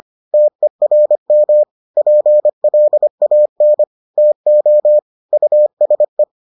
Morse code, heard as a keyed tone, about 25 wpm, 600 Hz. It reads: TERM PLAN TO USE